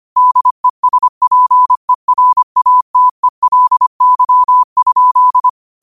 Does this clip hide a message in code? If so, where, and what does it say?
Yes: Morse code from 0.2 to 5.5 s, heard as a keyed tone, about 25 wpm, 1000 Hz. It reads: DESPERATELY?